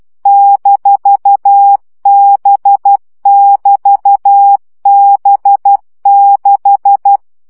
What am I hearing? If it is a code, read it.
Morse code, heard as a keyed tone, about 12 wpm, 800 Hz. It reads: -B<BT>B6
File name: CW gyakorlas b 6 egyenlo minusz 10 baud